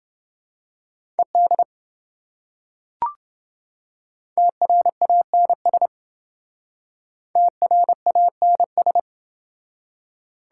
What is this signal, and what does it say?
Morse code, heard as a keyed tone, about 27 wpm, 700 Hz. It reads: ED TRANS TRANS